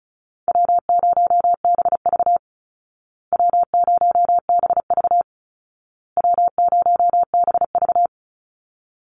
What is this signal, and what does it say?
Morse code, heard as a keyed tone, about 34 wpm, 700 Hz. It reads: W0BV W0BV W0BV
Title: Easy Call Signs Repeated 3x - 35wpm